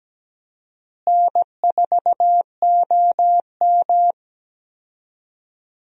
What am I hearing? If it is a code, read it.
Morse code, heard as a keyed tone, about 17 wpm, 700 Hz. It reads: N4OM
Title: USA Call Signs - 4 Characters - Repeated 2x - 17wpm